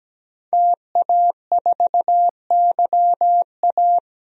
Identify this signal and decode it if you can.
Morse code, heard as a keyed tone, about 17 wpm, 700 Hz. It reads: TA4YA
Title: DXCC-100 Call Signs - 5 Characters - Repeated 3x - 17wpm